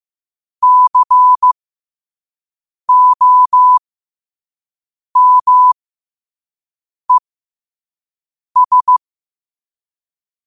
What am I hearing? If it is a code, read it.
Morse code, heard as a keyed tone, about 15 wpm, spaced as for 5 wpm, 1000 Hz. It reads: COMES
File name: qotdfiles / 5wpm